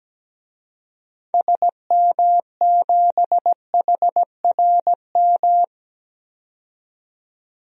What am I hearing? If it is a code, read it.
Morse code, heard as a keyed tone, about 17 wpm, 700 Hz. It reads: SM7HRM
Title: DXCC-100 Call Signs - 6 Characters - 17wpm